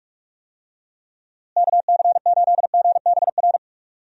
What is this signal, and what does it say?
Morse code, heard as a keyed tone, about 46 wpm, 700 Hz. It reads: KC8GBR
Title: USA Call Signs - 6 Characters - Repeated 3x - 45wpm